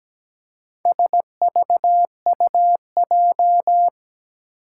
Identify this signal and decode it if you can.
Morse code, heard as a keyed tone, about 17 wpm, 700 Hz. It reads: SVUJ
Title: Four Challenging Letter-Number Combinations 17wpm